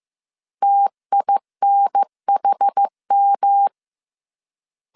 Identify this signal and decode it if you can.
Morse code, heard as a keyed tone, about 15 wpm, 800 Hz. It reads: TINHM
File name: CW / DET / 1 3 1500C S A 14wpm 2447